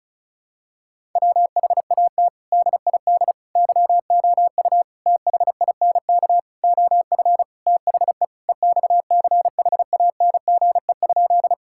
Morse code, heard as a keyed tone, about 34 wpm, 700 Hz. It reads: WHAT DID YOU THINK OF THE EXCHANGE?
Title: Sentences from Top 1000 Words - Review Part 6 - 35wpm